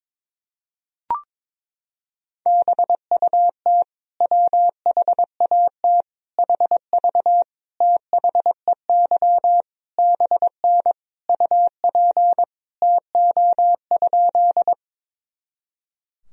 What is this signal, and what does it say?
Morse code, heard as a keyed tone, about 22 wpm, 700 Hz. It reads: BUT WHAT HV THEY BN UP TO?
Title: Sentences from Top 100 Words and Abbreviations 22wpm